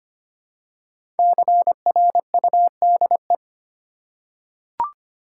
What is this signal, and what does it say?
Morse code, heard as a keyed tone, about 25 wpm, 700 Hz. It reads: CRUDE
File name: 5 Letter Words - First 3 Letters Given - 25wpm